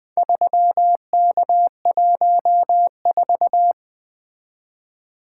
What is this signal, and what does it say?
Morse code, heard as a keyed tone, about 20 wpm, 700 Hz. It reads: 3K14
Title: Four Letter-Number Combinations - 20wpm